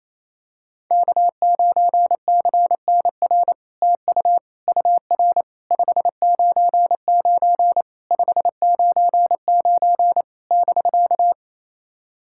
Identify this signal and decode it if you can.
Morse code, heard as a keyed tone, about 29 wpm, 700 Hz. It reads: K9CNR TU UR 599 599 <BK>
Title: SOTA QSO Practice 28wpm